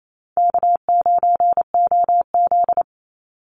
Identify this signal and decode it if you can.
Morse code, heard as a keyed tone, about 28 wpm, 700 Hz. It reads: K9OZ
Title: Easy Call Signs - 28wpm